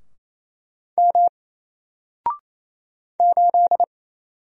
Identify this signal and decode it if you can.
Morse code, heard as a keyed tone, about 27 wpm, 700 Hz. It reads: M 8
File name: Single-Letter-Number-28wpm